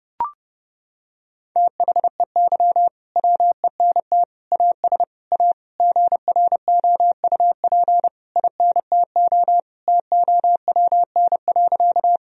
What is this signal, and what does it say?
Morse code, heard as a keyed tone, about 32 wpm, 700 Hz. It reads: THEY WENT AS A GROUP INTO TOWN.
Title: Sentences from Top 1000 Words - Review Part 1 - 30wpm